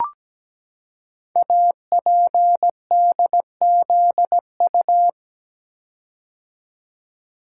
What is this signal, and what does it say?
Morse code, heard as a keyed tone, about 17 wpm, 700 Hz. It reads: APDZU
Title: Code Groups - 5 Letters - 17wpm